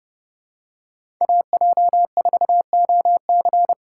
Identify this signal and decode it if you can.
Morse code, heard as a keyed tone, about 30 wpm, 700 Hz. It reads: AJ4OC
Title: USA Call Signs - 5 Characters - 3x Character Spacing - 30wpm